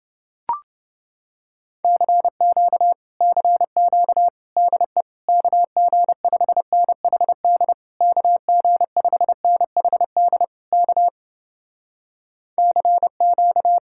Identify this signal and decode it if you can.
Morse code, heard as a keyed tone, about 31 wpm, 700 Hz. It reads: CQ CQ DE KG5NHD KG5NHD K CQ